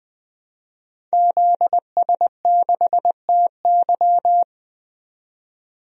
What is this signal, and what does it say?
Morse code, heard as a keyed tone, about 20 wpm, 700 Hz. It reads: ZS6TY
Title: DXCC-100 Call Signs - 5 Characters - 20wpm